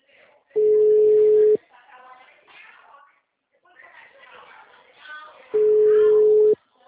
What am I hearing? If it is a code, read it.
Ringback tone (Europe, 425 Hz), with speech.